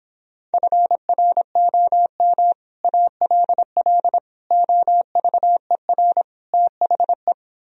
Morse code, heard as a keyed tone, about 31 wpm, 700 Hz. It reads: FROM ALL OVER THE